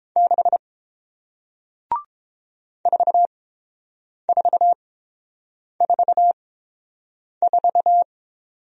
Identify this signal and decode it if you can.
Morse code, heard as a keyed tone, about 30 wpm, spaced as for 7 wpm, 700 Hz. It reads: 6 4444